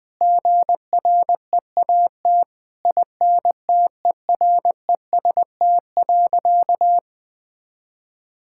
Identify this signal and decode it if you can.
Morse code, heard as a keyed tone, about 20 wpm, 700 Hz. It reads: GREAT INTEREST.
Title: Sentences from Top 1000 Words - Review Part 6 - 20wpm